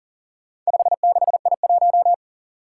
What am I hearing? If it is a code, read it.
Morse code, heard as a keyed tone, about 39 wpm, 700 Hz. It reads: 56I1
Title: Four Challenging Letter-Number Combinations 40wpm